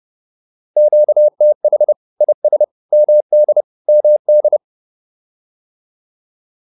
Morse code, heard as a keyed tone, about 30 wpm, 600 Hz. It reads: QTH IS MD MD
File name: QSO101 30